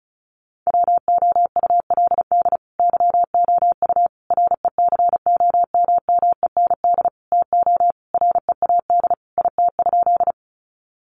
Morse code, heard as a keyed tone, about 34 wpm, 700 Hz. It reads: WOULD YOU RECOMMEND TO READ IT?
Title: Sentences from Top 1200 Words 35wpm